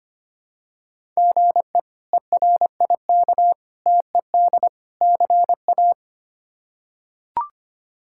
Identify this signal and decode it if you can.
Morse code, heard as a keyed tone, about 25 wpm, 700 Hz. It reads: GE ERIK TED CA
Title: K1USN SST QSO Practice - 25wpm